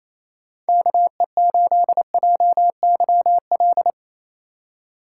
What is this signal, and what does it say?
Morse code, heard as a keyed tone, about 34 wpm, 700 Hz. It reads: KE8JYL
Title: Speed Racing - USA Call Signs 28wpm